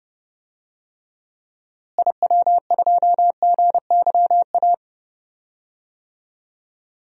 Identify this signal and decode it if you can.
Morse code, heard as a keyed tone, about 31 wpm, 700 Hz. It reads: IW2GYA
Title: DXCC-100 Call Signs - 6 Characters - Repeated 2x - 30wpm